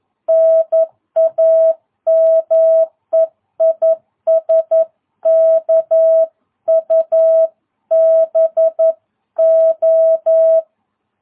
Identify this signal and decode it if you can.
Morse code, heard as a keyed tone, about 11 wpm, 650 Hz. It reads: NAME IS KUBO
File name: jh8cbh / 2022cw